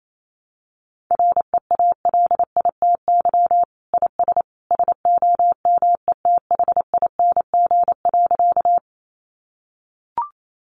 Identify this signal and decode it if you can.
Morse code, heard as a keyed tone, about 28 wpm, 700 Hz. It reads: REALITY IS SOMETHING.
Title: Sentences from Top 1200 Words 28wpm